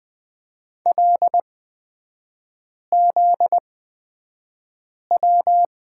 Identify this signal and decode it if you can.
Morse code, heard as a keyed tone, about 20 wpm, spaced as for 5 wpm, 700 Hz. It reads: LZW
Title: 243 Lesson 39e - Letter z - Three Characters 20wpm